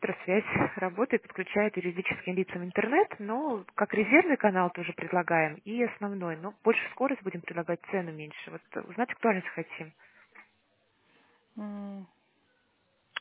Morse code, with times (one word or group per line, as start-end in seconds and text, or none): none